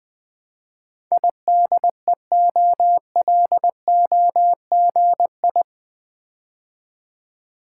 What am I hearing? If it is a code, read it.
Morse code, heard as a keyed tone, about 20 wpm, 700 Hz. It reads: IDEOLOGI
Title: Build A Word - 4 Letters At A Time - 11-Letter Words - 20wpm